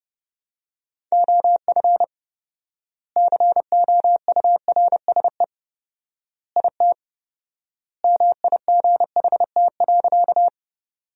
Morse code, heard as a keyed tone, about 32 wpm, 700 Hz. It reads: OF COURSE IT MIGHT.
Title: 4x Word-Spacing - Sentences from Top 500 Words - 30wpm